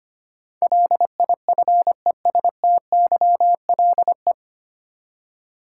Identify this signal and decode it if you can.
Morse code, heard as a keyed tone, about 25 wpm, 700 Hz. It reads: LIFESTYLE
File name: practice / 25wpm / 9 Letter Words - First 3 Letters Given - 25wpm